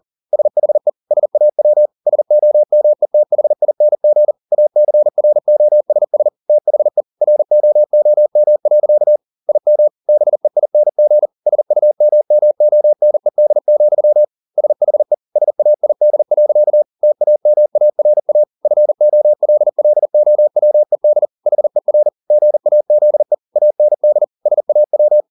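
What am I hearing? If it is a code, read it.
Morse code, heard as a keyed tone, about 40 wpm, 600 Hz. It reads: SHE SAW SOMETHING ACROSS THE ROOM. IM BEING SUMMONED, SHE SAID. TAMARA FOLLOWED HER GAZE AND SAW
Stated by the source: Morse code at 40 wpm, practice text, CW med Gnister.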